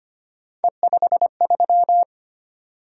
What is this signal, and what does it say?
Morse code, heard as a keyed tone, about 25 wpm, 700 Hz. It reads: E53
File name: Three Letter-Number Combinations - 25wpm